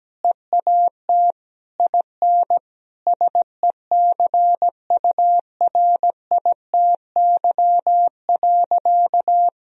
Morse code, heard as a keyed tone, about 17 wpm, 700 Hz. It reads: EAT IN SECURITY.
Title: Sentences from Top 1000 Words - Review Part 4 - 17wpm